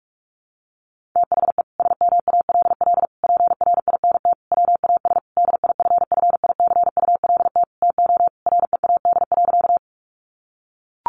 Morse code, heard as a keyed tone, about 44 wpm, 700 Hz. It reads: THE SMALL PRINT WAS DIFFICULT TO READ.
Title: Sentences from Top 1400 Words 45wpm